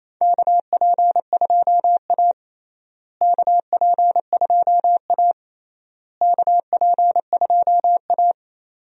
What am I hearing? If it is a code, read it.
Morse code, heard as a keyed tone, about 28 wpm, 700 Hz. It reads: KP2A KP2A KP2A
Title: DXCC-100 Call Signs - 4 Characters - Repeated 3x - 28wpm